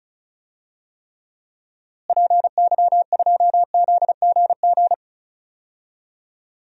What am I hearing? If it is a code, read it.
Morse code, heard as a keyed tone, about 36 wpm, 700 Hz. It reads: PY2ZGG
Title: DXCC-100 Call Signs - 6 Characters - Repeated 2x - 35wpm